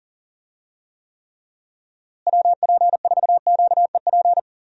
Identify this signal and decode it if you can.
Morse code, heard as a keyed tone, about 40 wpm, 700 Hz. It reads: WP4QEP
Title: USA Call Signs - 6 Characters - Repeated 3x - 40wpm